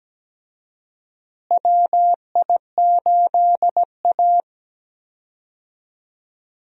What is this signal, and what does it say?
Morse code, heard as a keyed tone, about 17 wpm, 700 Hz. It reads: WI8A
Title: USA Call Signs - 4 Characters - Repeated 3x - 17wpm